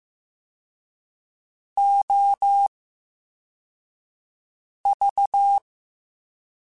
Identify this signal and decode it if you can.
Morse code, heard as a keyed tone, about 15 wpm, spaced as for 4 wpm, 800 Hz. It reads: OV
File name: Special INGs-mp3s / Special INGs-mp3s / VerySlowToFaster Ing @13